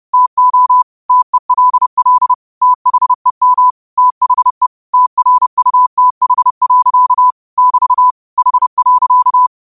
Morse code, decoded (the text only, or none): TO TELL THEM THE TRUTH. <BT> H.